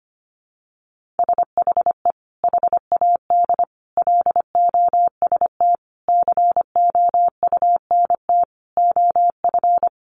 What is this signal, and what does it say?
Morse code, heard as a keyed tone, about 25 wpm, 700 Hz. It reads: SHE HAD LOST COUNT OF
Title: Sentences from Top 1500 Words 25wpm